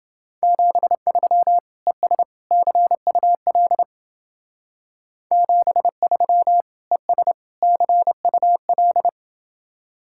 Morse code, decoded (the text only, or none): 73 ES CUL 73 ES CUL